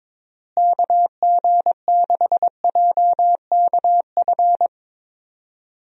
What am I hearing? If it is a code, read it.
Morse code, heard as a keyed tone, about 21 wpm, 700 Hz. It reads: KG6JKF
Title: USA Call Signs - 6 Characters - Repeated 2x - 22wpm